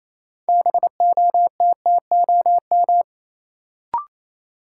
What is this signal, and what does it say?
Morse code, heard as a keyed tone, about 28 wpm, 700 Hz. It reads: BOTTOM
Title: Sets of 3 Words - New Word Drill - 28wpm